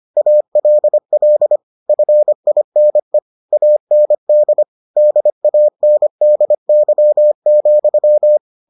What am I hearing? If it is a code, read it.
Morse code, heard as a keyed tone, about 25 wpm, 600 Hz. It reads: ALL FINE AND DANDY,